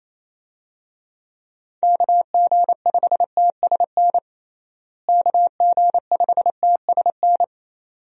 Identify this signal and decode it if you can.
Morse code, heard as a keyed tone, about 28 wpm, 700 Hz. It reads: KG5TSN KG5TSN
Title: USA Call Signs - 6 Characters - Repeated 2x - 28wpm